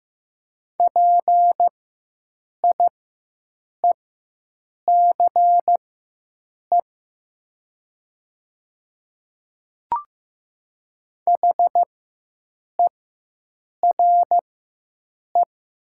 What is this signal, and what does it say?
Morse code, heard as a keyed tone, about 15 wpm, spaced as for 7 wpm, 700 Hz. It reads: PIECE HERE
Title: 4x Character Spacing - Top 500 Words - 15wpm